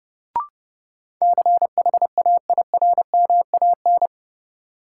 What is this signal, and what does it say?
Morse code, heard as a keyed tone, about 30 wpm, 700 Hz. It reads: CHAIRMAN
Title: Sets of 3 Words - New Word Drill - 30wpm